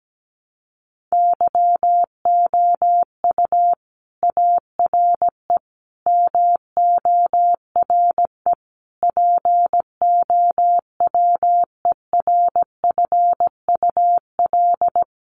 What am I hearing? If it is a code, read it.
Morse code, heard as a keyed tone, about 17 wpm, 700 Hz. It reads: YOU ARE MORE POWERFUL
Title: Sentences from Top 1300 Words 17wpm